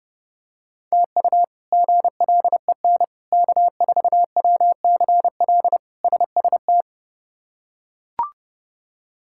Morse code, heard as a keyed tone, about 30 wpm, 700 Hz. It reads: TU GLEN K4WCL SST
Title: K1USN SST QSO Practice - 30wpm